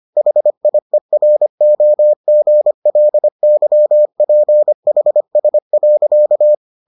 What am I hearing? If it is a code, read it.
Morse code, heard as a keyed tone, about 25 wpm, 600 Hz. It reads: HIEROGLYPHS.